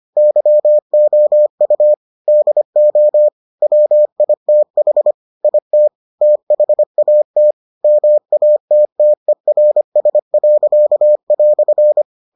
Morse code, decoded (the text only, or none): YOU DO WITH IT THAT MATTERS."